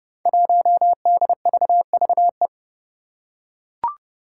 Morse code, heard as a keyed tone, about 30 wpm, 700 Hz. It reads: 1DVVE
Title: Five Challenging Letter-Number Combinations 30wpm